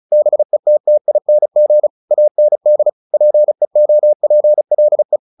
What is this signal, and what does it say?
Morse code, heard as a keyed tone, about 34 wpm, 600 Hz. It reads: BETTING AND PEOPLE
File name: the old man and the sea 2 35wpm